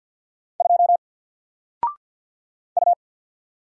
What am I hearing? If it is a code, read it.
Morse code, heard as a keyed tone, about 48 wpm, 700 Hz. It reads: 2 U
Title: Single Letter-Number Alt 50wpm